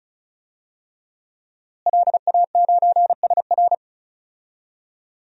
Morse code, heard as a keyed tone, about 36 wpm, 700 Hz. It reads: LA9SR
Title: DXCC-100 Call Signs - 5 Characters - Repeated 3x - 35wpm